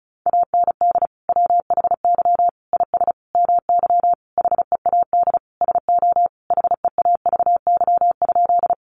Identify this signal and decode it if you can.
Morse code, heard as a keyed tone, about 34 wpm, 700 Hz. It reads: AND WHY IS MY HEAD SO HEAVY?
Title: Sentences from Top 1100 Words 35wpm